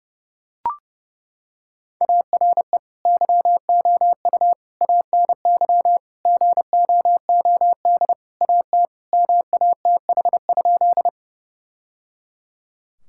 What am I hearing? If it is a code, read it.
Morse code, heard as a keyed tone, about 30 wpm, 700 Hz. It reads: ARE YOU ANY GOOD AT MATH?
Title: Sentences from Top 1000 Words - Review Part 6 - 30wpm